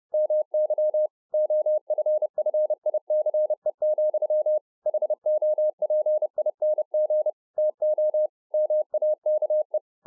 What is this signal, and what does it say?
Morse code, heard as a keyed tone, about 30 wpm, 610 Hz. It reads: MY OFFICE, HOPING TO MAKE